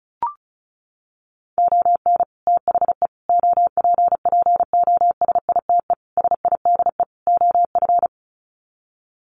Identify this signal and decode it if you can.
Morse code, heard as a keyed tone, about 34 wpm, 700 Hz. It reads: ON THE OPPOSITE SIDE OF
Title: Sets of 5 Words Encore2 35wpm